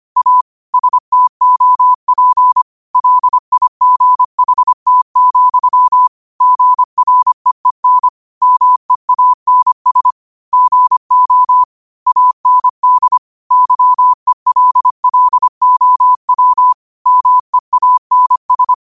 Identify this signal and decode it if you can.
Morse code, heard as a keyed tone, about 25 wpm, 1000 Hz. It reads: A STOP LIGHT, GREEN MEANS GO AND YELLOW MEANS